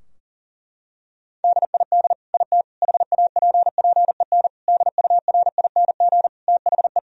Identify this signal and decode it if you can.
Morse code, heard as a keyed tone, about 40 wpm, 700 Hz. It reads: DID IT HAPPEN DURING THE
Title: Sentences from Top 1000 Words - Review Part 2 - 40wpm